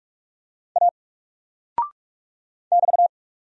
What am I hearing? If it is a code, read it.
Morse code, heard as a keyed tone, about 44 wpm, 700 Hz. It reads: A <BT>
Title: Single Character (Letter:Number:Punctuation:Prosign) 45wpm